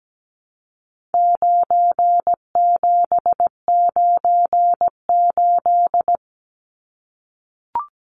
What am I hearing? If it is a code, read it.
Morse code, heard as a keyed tone, about 17 wpm, 700 Hz. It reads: 9798
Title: Numbers - 4-Digit 17wpm